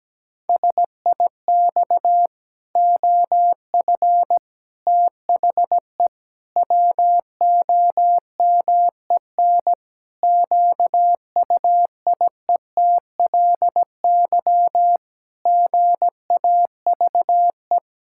Morse code, heard as a keyed tone, about 17 wpm, 700 Hz. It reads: SIX OF THE WOMEN QUIETLY GAVE
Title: Pangram Challenge v2 - 17wpm